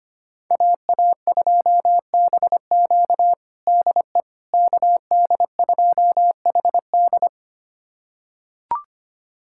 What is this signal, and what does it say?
Morse code, heard as a keyed tone, about 25 wpm, 700 Hz. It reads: AA2BQ DE KD2HD
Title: USA Callsign DE Callsign - 5 Characters - 25wpm